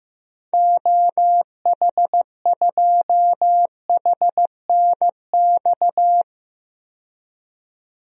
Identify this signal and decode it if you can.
Morse code, heard as a keyed tone, about 15 wpm, 700 Hz. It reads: OH2HNX
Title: DXCC-100 Call Signs - 6 Characters - 15wpm